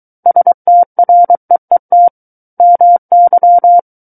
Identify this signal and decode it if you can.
Morse code, heard as a keyed tone, about 23 wpm, 700 Hz. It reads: STREET MY
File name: Book One / 14-Chapter 14 - 23 WPM